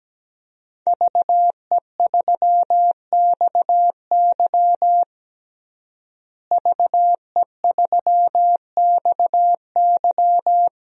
Morse code, heard as a keyed tone, about 17 wpm, 700 Hz. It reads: VE3XY VE3XY
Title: DXCC-100 Call Signs - 5 Characters - Repeated 3x - 17wpm